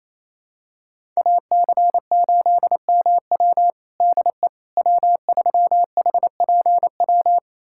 Morse code, heard as a keyed tone, about 28 wpm, 700 Hz. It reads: AC8MW DE W3HPW
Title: USA Callsign DE Callsign - 5 Characters - 28wpm